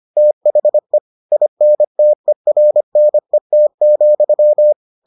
Morse code, heard as a keyed tone, about 25 wpm, 600 Hz. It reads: THE INTERNET,